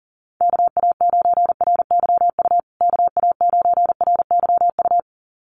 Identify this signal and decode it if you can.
Morse code, heard as a keyed tone, about 40 wpm, 700 Hz. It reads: KA9RYU KA9RYU
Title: USA Call Signs Repeated 2x 40wpm